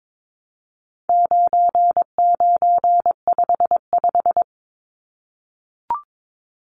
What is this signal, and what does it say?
Morse code, heard as a keyed tone, about 22 wpm, 700 Hz. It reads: 9955
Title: Numbers - 4-Digit 22wpm